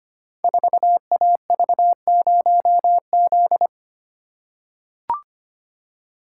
Morse code, heard as a keyed tone, about 25 wpm, 700 Hz. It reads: 4AV0Z
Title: Code Groups - 5 Letter:Numbers - 25wpm